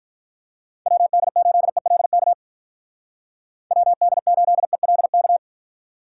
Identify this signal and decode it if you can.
Morse code, heard as a keyed tone, about 44 wpm, 700 Hz. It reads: WD8ELK WD8ELK